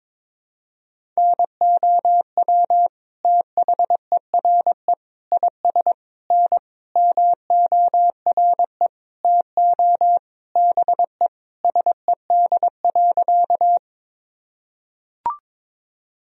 Morse code, heard as a keyed tone, about 22 wpm, 700 Hz. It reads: NOW THERE IS N MORE TO BE SED.
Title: Sentences from Top 100 Words and Abbreviations 22wpm